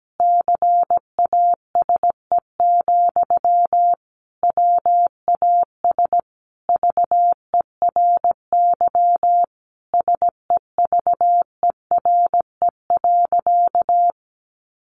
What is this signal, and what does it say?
Morse code, heard as a keyed tone, about 17 wpm, 700 Hz. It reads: CASE, WAS VERY SEVERE.
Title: Sentences from Top 1400 Words 17wpm